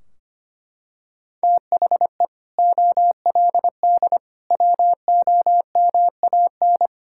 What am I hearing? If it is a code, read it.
Morse code, heard as a keyed tone, about 25 wpm, 700 Hz. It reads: THE OLD WOMAN
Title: Sentences from Top 1600 Words 25wpm